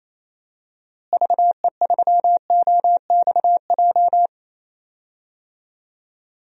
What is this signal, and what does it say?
Morse code, heard as a keyed tone, about 28 wpm, 700 Hz. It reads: VE3OXJ
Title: DXCC-100 Call Signs - 6 Characters - Repeated 3x - 28wpm